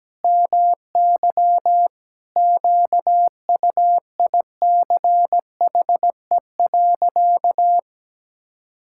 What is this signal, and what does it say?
Morse code, heard as a keyed tone, about 17 wpm, 700 Hz. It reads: MY QUICHE.
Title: Pangram Challenge v2 - 17wpm